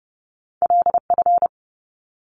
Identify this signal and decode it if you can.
Morse code, heard as a keyed tone, about 30 wpm, 700 Hz. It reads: LF
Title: Top Two Letter Combinations in Words 30wpm